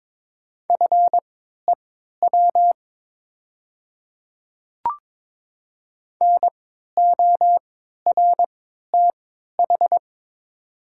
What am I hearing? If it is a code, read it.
Morse code, heard as a keyed tone, about 22 wpm, spaced as for 13 wpm, 700 Hz. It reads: FEW NORTH
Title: 3x Character Spacing - Top 300 Words - 22wpm